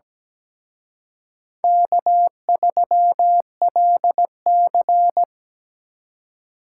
Morse code, heard as a keyed tone, about 20 wpm, 700 Hz. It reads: K3LC
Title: Speed Racing - USA 1x2 Call Signs 17wpm